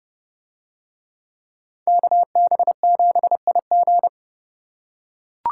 Morse code, heard as a keyed tone, about 29 wpm, 700 Hz. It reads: KB7IG